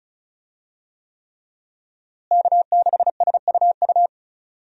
Morse code, heard as a keyed tone, about 36 wpm, 700 Hz. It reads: K6SUU
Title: USA Call Signs - 5 Characters - Repeated 3x - 35wpm